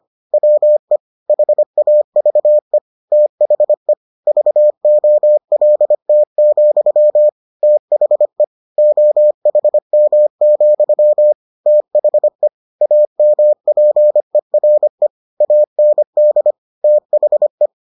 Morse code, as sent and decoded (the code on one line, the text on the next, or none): .-- . / .... .- ...- . / - .... / ...- --- .-.. - --..-- / - .... / --- .... -- --..-- / - .... / .- -- .--. .-. / .- -. -.. / - .... .
WE HAVE THE VOLT, THE OHM, THE AMPERE AND THE